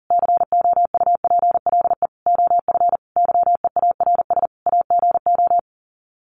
Morse code, heard as a keyed tone, about 40 wpm, 700 Hz. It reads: COUPLE OF YEARS AGO
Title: Sets of 5 Words 40wpm